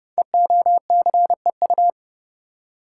Morse code, heard as a keyed tone, about 29 wpm, 700 Hz. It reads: EOCEU
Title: Code Groups - 5 Letter:Numbers - 30wpm